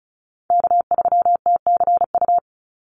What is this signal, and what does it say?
Morse code, heard as a keyed tone, about 34 wpm, 700 Hz. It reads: K3TCU
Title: NA SOTA Chaser Callsigns 35wpm